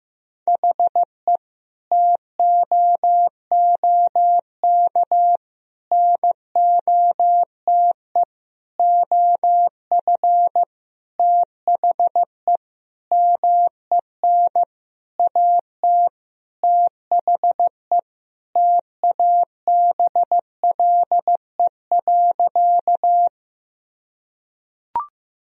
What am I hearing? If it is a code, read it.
Morse code, heard as a keyed tone, about 15 wpm, 700 Hz. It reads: HE TOOK NOTE OF THE MEN AT THE TABLE.
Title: Sentences from Top 1000 Words - Review Part 2 - 15wpm